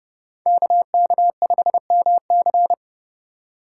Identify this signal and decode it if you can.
Morse code, heard as a keyed tone, about 35 wpm, 700 Hz. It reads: KK5MC
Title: Speed Racing - USA 2x2 Call Signs 30wpm